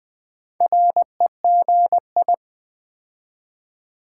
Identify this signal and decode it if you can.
Morse code, heard as a keyed tone, about 20 wpm, 700 Hz. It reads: REGI
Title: Build A Word - 4 Letters At A Time - 12-Letter Words - 20wpm